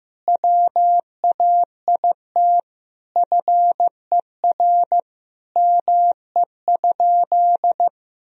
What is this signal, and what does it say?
Morse code, heard as a keyed tone, about 15 wpm, 700 Hz. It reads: WAIT FER ME?
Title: Sentences from Top 500 Words and Abbreviations 15wpm